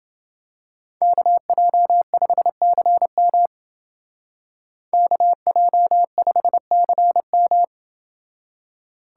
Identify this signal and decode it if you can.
Morse code, heard as a keyed tone, about 28 wpm, 700 Hz. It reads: KJ5CM KJ5CM